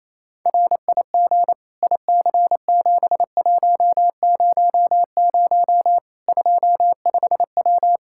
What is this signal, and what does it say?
Morse code, heard as a keyed tone, about 28 wpm, 700 Hz. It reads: RIG IC7100 25W
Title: K1USN SST QSO Practice 28wpm